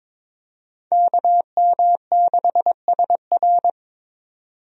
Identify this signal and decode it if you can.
Morse code, heard as a keyed tone, about 22 wpm, 700 Hz. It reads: KM6SR
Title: USA Call Signs - 5 Characters - 22wpm